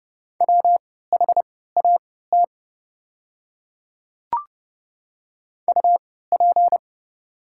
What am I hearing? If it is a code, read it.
Morse code, heard as a keyed tone, about 29 wpm, spaced as for 17 wpm, 690 Hz. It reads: WHAT UP
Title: 3x Character Spacing - Top 100 Words - 30wpm